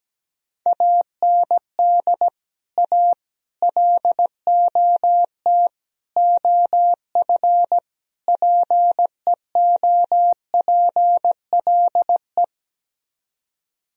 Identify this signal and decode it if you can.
Morse code, heard as a keyed tone, about 17 wpm, 700 Hz. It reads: AND A LOT OF PEOPLE
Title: Speed-Racing - Sets of 5 Words 17wpm